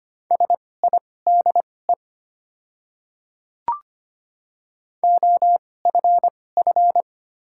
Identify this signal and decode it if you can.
Morse code, heard as a keyed tone, about 25 wpm, spaced as for 18 wpm, 700 Hz. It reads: SIDE OFF
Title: 2x Character Spacing - Top 200 Words - 25wpm